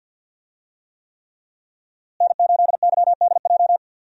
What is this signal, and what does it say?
Morse code, heard as a keyed tone, about 50 wpm, 700 Hz. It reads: N8YDJ
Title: USA Call Signs - 5 Characters - Repeated 3x - 50wpm